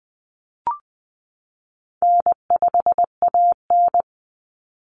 Morse code, heard as a keyed tone, about 20 wpm, 700 Hz. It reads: N5AN
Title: Easy Call Signs Repeated 3x - 20wpm